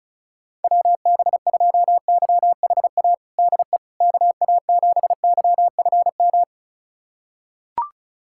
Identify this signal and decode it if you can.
Morse code, heard as a keyed tone, about 36 wpm, 700 Hz. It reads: WB2YHA DE KA7YFM
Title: USA Callsign DE Callsign - 6 Characters - 35wpm